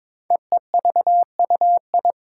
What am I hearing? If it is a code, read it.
Morse code, heard as a keyed tone, about 22 wpm, 700 Hz. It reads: EEVUI
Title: Code Groups - 5 Letters - 22wpm